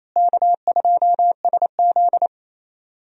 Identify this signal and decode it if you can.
Morse code, heard as a keyed tone, about 28 wpm, 700 Hz. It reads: K2SZ
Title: USA Call Signs - 4 Characters - Repeated 2x - 28wpm